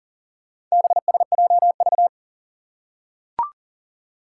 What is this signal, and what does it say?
Morse code, heard as a keyed tone, about 40 wpm, 700 Hz. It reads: BSJV
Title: Four Challenging Letter-Number Combinations 40wpm